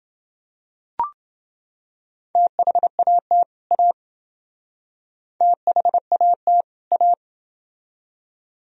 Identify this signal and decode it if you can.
Morse code, heard as a keyed tone, about 28 wpm, 700 Hz. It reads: THAT A THAT A